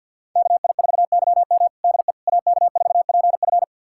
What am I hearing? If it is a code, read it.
Morse code, heard as a keyed tone, about 51 wpm, 700 Hz. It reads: KI4YM DE AK4PF
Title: USA Callsign DE Callsign - 5 Characters - 50wpm